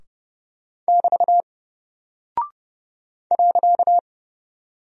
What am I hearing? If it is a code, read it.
Morse code, heard as a keyed tone, about 30 wpm, 700 Hz. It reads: <BT> .
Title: Single Character (Letter:Number:Punctuation:Prosign) Alt 30wpm